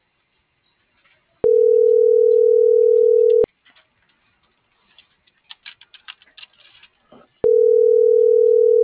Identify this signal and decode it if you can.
Ringback tone (North America), 440 + 480 Hz.